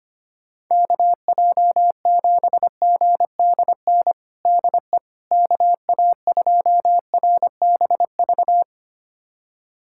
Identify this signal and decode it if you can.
Morse code, heard as a keyed tone, about 25 wpm, 700 Hz. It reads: KJ7GDN DE KA2RBV
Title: USA Callsign DE Callsign - 6 Characters - 25wpm